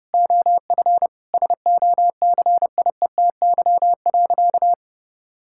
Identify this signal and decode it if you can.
Morse code, heard as a keyed tone, about 32 wpm, 700 Hz. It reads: OF SOCIETY.
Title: Sentences from Top 1000 Words - Review Part 6 - 30wpm